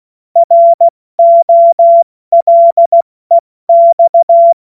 Morse code, heard as a keyed tone, about 16 wpm, 680 Hz. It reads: ROLEX
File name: Morsekurs / Witze / Witz 17. Mai 2022 70 F80